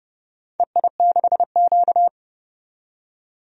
Morse code, heard as a keyed tone, about 30 wpm, 700 Hz. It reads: EI6Q
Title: DXCC-100 Call Signs - 4 Characters - 30wpm